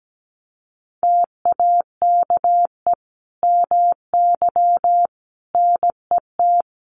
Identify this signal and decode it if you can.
Morse code, heard as a keyed tone, about 17 wpm, 700 Hz. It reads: TAKE MY NET
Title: Sentences from Top 1400 Words 17wpm